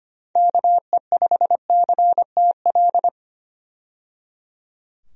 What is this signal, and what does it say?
Morse code, heard as a keyed tone, about 25 wpm, 700 Hz. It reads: KE5CTL
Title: USA Call Signs - 6 Characters - 25wpm